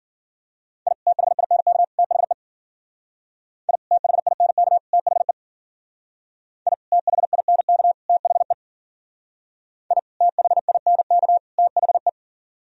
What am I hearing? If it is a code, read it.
Morse code, heard as a keyed tone, about 49 wpm, 700 Hz. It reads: I THINK THE I THINK THE I THINK THE I THINK THE
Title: Speed-Racing - Sets of 3 Words 40wpm